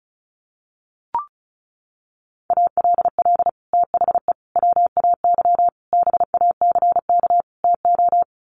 Morse code, heard as a keyed tone, about 34 wpm, 700 Hz. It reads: ALL THE WAY BACK TO
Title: Sets of 5 Words Encore 35wpm